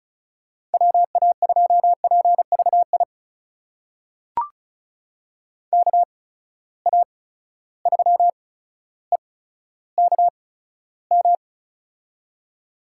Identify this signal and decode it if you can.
Morse code, heard as a keyed tone, about 34 wpm, spaced as for 15 wpm, 700 Hz. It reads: WA2PVI KA3EKM
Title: USA Call Signs - 6 Characters - 8x Character Spacing - 35wpm